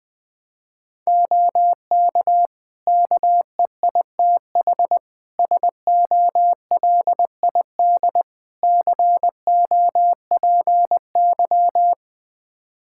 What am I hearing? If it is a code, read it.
Morse code, heard as a keyed tone, about 21 wpm, 700 Hz. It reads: OK KEITH SOLID COPY
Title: Practice QSO - Exchange 3 - More Info - 20wpm